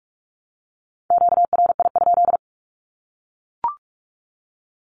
Morse code, heard as a keyed tone, about 44 wpm, 700 Hz. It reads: QRI?
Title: Q-Codes 45wpm